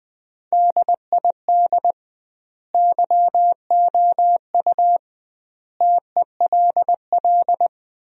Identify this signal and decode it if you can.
Morse code, heard as a keyed tone, about 20 wpm, 700 Hz. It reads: DID YOU TELL
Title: 2x Word-Spacing - Sentences from Top 200 Words - 20wpm